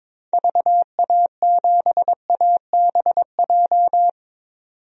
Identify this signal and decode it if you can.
Morse code, heard as a keyed tone, about 22 wpm, 700 Hz. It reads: VA7ABJ
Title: DXCC-100 Call Signs - 6 Characters - 22wpm